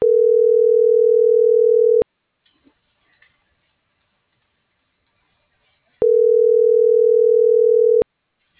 Ringback tone (North America), 440 + 480 Hz.